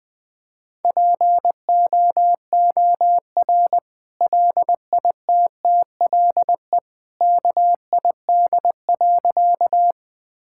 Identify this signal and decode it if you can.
Morse code, heard as a keyed tone, about 20 wpm, 700 Hz. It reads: POOR LITTLE KID.
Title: Sentences from Top 1600 Words 20wpm